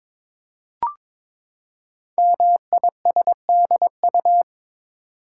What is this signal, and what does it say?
Morse code, heard as a keyed tone, about 22 wpm, 700 Hz. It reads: MISDU
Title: Code Groups - 5 Letters - 22wpm